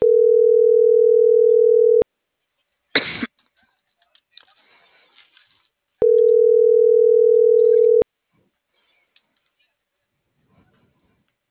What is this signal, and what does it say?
Ringback tone (North America), 440 + 480 Hz.